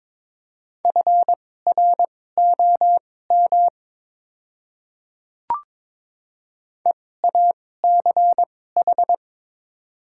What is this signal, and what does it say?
Morse code, heard as a keyed tone, about 22 wpm, spaced as for 16 wpm, 690 Hz. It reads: FROM EACH
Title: 2x Character Spacing - Top 100 Words - 22wpm